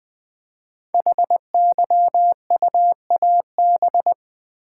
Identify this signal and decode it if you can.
Morse code, heard as a keyed tone, about 20 wpm, 700 Hz. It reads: HYUAB
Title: Code Groups - 5 Letter:Numbers - 20wpm